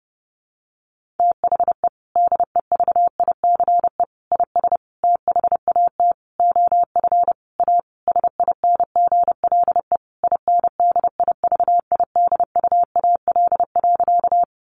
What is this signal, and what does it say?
Morse code, heard as a keyed tone, about 30 wpm, 700 Hz. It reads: THE DEVICE IS THAT OF A SINGLE INDIVIDUAL.
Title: Sentences from Top 1500 Words 30wpm